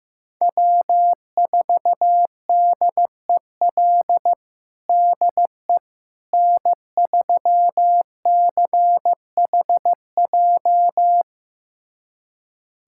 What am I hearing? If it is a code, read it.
Morse code, heard as a keyed tone, about 15 wpm, 700 Hz. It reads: W4DEL DE N3CHJ
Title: USA Callsign DE Callsign - 5 Characters - 15wpm